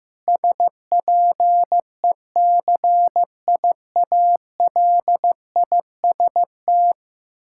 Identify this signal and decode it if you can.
Morse code, heard as a keyed tone, about 16 wpm, 700 Hz. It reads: SPECIALIST
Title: Practice QSO - Exchange 3 - More Info - 15wpm